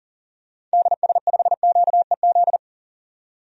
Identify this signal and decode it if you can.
Morse code, heard as a keyed tone, about 40 wpm, 700 Hz. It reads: DS5QEZ
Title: DXCC-100 Call Signs - 6 Characters - 40wpm